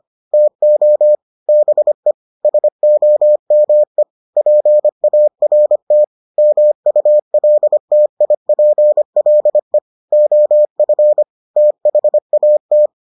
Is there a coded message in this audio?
Yes: Morse code from 0.3 to 12.9 s, heard as a keyed tone, about 25 wpm, 600 Hz. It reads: TO BE SOME PART MULTIPLE OF THAT